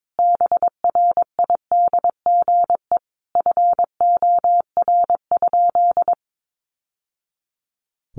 Morse code, heard as a keyed tone, about 22 wpm, 700 Hz. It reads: BRIDGE FOR?
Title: Sentences from Top 1200 Words 22wpm